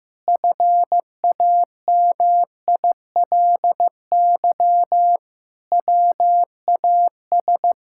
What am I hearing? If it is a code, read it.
Morse code, heard as a keyed tone, about 15 wpm, 700 Hz. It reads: FAMILY WAS
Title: Sentences from Top 1000 Words - Review Part 6 - 15wpm